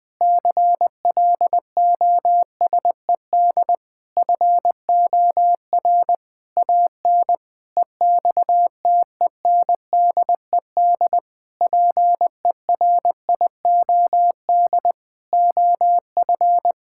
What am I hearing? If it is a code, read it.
Morse code, heard as a keyed tone, about 20 wpm, 700 Hz. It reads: CLOSED FOR AN EXTENDED PERIOD OF
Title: Sentences with 6-Word Phrases - 20wpm